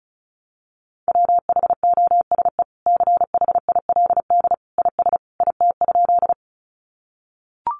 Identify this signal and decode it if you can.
Morse code, heard as a keyed tone, about 34 wpm, 700 Hz. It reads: WHOSE CHILD IS IT?
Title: Sentences from Top 1200 Words 35wpm